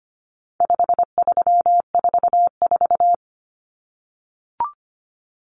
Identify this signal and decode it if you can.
Morse code, heard as a keyed tone, about 25 wpm, 700 Hz. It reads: 5344
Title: Numbers - 4-Digit 25wpm